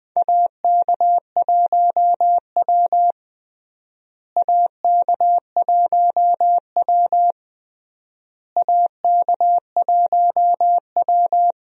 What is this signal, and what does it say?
Morse code, heard as a keyed tone, about 20 wpm, 700 Hz. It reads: AK1W AK1W AK1W
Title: USA Call Signs - 4 Characters - Repeated 3x - 20wpm